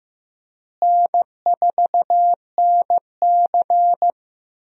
Morse code, heard as a keyed tone, about 15 wpm, 700 Hz. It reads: N4NC
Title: USA Call Signs - 4 Characters - Repeated 3x - 15wpm